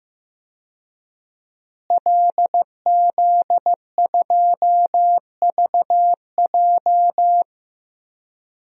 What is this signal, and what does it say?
Morse code, heard as a keyed tone, about 15 wpm, 700 Hz. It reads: LZ2VJ
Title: DXCC-100 Call Signs - 5 Characters - Repeated 2x - 15wpm